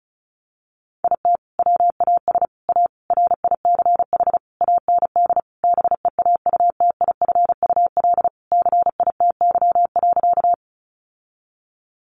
Morse code, heard as a keyed tone, about 34 wpm, 700 Hz. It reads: IT WAS A RICH AND BEAUTIFUL CITY.
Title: Sentences from Top 1100 Words 35wpm